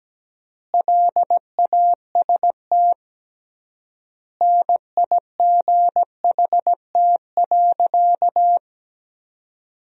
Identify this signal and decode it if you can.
Morse code, heard as a keyed tone, about 17 wpm, 700 Hz. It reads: LAST NIGHT.
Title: 3x Word-Spacing - Sentences from Top 300 Words - 17wpm